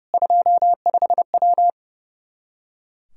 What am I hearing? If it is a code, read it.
Morse code, heard as a keyed tone, about 32 wpm, 700 Hz. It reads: 25W
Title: Practice QSO - Exchange 2 - Info - 30wpm